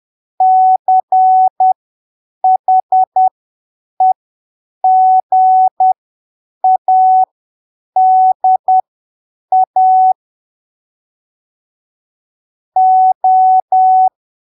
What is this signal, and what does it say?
Morse code, heard as a keyed tone, about 10 wpm, spaced as for 7 wpm, 750 Hz. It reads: CHEGADA O
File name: arquivos / texto / texto 19 10ppm